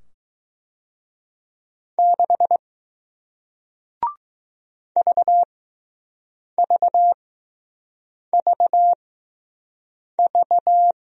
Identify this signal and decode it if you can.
Morse code, heard as a keyed tone, about 20 wpm, spaced as for 6 wpm, 700 Hz. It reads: 6 VVVV